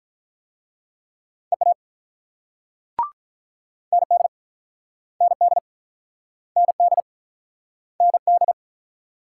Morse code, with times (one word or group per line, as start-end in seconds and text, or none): 1.5-1.7 EA
3.9-4.3 ND
5.2-5.6 ND
6.6-7.0 ND
8.0-8.5 ND